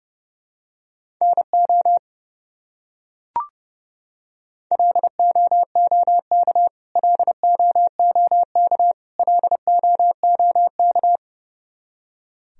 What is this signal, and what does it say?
Morse code, heard as a keyed tone, about 32 wpm, 700 Hz. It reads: NO LOOK LOOK LOOK
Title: Top 100 Words Repeated 30wpm